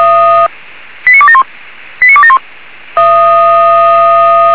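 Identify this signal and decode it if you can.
Five-tone selective call (ZVEI), heard as quick runs of five tones, tones of about 70 ms. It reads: 89271 89271
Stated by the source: ZVEI five-tone selective call received on an SDR.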